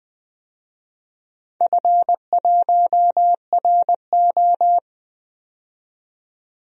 Morse code, heard as a keyed tone, about 20 wpm, 700 Hz. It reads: F1RO
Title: DXCC-100 Call Signs - 4 Characters - 20wpm